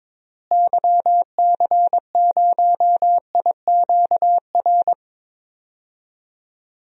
Morse code, heard as a keyed tone, about 22 wpm, 700 Hz. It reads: YC0IQR
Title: DXCC-100 Call Signs - 6 Characters - Repeated 2x - 22wpm